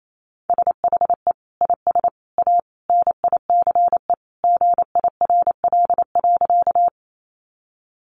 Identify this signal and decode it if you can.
Morse code, heard as a keyed tone, about 28 wpm, 700 Hz. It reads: SHE IS A NICE GIRL.
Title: Sentences from Top 1100 Words 28wpm